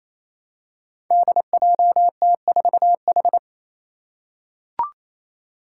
Morse code, heard as a keyed tone, about 28 wpm, 700 Hz. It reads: DJT4H
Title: Five Challenging Letter-Number Combinations 28wpm